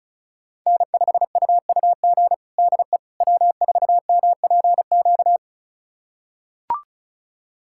Morse code, heard as a keyed tone, about 36 wpm, 700 Hz. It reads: N5UUG DE W4MPQ
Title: USA Callsign DE Callsign - 5 Characters - 35wpm